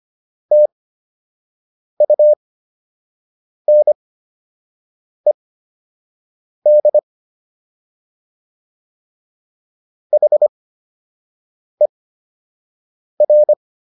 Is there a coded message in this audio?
Yes: Morse code from 0.5 to 13.5 s, heard as a keyed tone, about 25 wpm, spaced as for 6 wpm, 600 Hz. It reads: TUNED HER